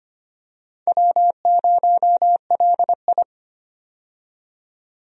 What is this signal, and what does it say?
Morse code, heard as a keyed tone, about 25 wpm, 700 Hz. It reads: W0LI
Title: USA Call Signs - 4 Characters - 25wpm